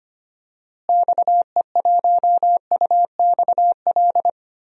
Morse code, heard as a keyed tone, about 25 wpm, 700 Hz. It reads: XE1UXL
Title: DXCC-100 Call Signs - 6 Characters - Repeated 2x - 25wpm